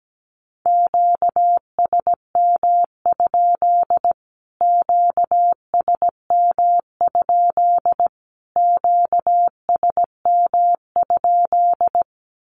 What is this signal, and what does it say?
Morse code, heard as a keyed tone, about 17 wpm, 700 Hz. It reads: QSM? QSM? QSM?
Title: Q-Codes Repeated 3x 17wpm